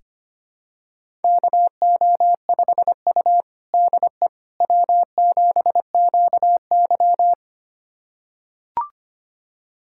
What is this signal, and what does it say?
Morse code, heard as a keyed tone, about 25 wpm, 700 Hz. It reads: KO5U DE W7QY
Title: USA Callsign DE Callsign - 4 Characters - 25wpm